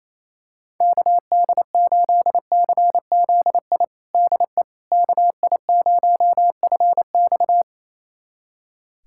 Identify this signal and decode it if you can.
Morse code, heard as a keyed tone, about 28 wpm, 700 Hz. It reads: KD8CZI DE KI0FX
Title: USA Callsign DE Callsign 28wpm